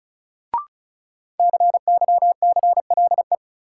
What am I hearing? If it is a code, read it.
Morse code, heard as a keyed tone, about 36 wpm, 700 Hz. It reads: CYCLE
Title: Sentences from Top 900 Words - New Word Drill - 35wpm